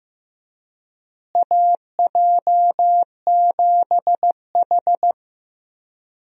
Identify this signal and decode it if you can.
Morse code, heard as a keyed tone, about 15 wpm, 700 Hz. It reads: AJ7H
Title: USA Call Signs - 4 Characters - Repeated 2x - 15wpm